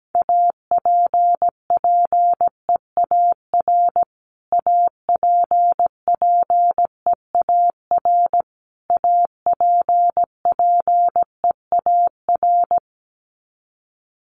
Morse code, heard as a keyed tone, about 17 wpm, 700 Hz. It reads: APPEAR APPEAR APPEAR
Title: Top 500 Words Repeated 17wpm